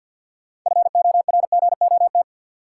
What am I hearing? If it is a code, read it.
Morse code, heard as a keyed tone, about 50 wpm, 700 Hz. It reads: FORGOT
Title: Sentences from Top 1400 Words - New Word Drill - 50wpm